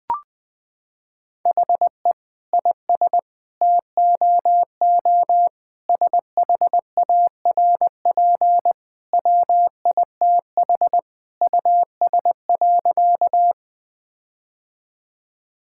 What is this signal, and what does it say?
Morse code, heard as a keyed tone, about 20 wpm, 700 Hz. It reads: HE IS TOO SHARP WITH US.
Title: Sentences from Top 1600 Words 20wpm